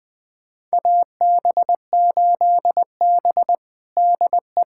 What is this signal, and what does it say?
Morse code, heard as a keyed tone, about 20 wpm, 700 Hz. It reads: AB8B DE
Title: USA Callsign DE Callsign - 4 Characters - 20wpm